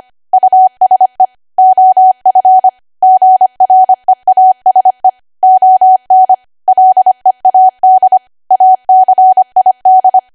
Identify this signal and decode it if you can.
Morse code, heard as a keyed tone, about 25 wpm, 750 Hz. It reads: USE OF GREASE ON LEAD ACID
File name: Archive / 25 WPM / 25WPM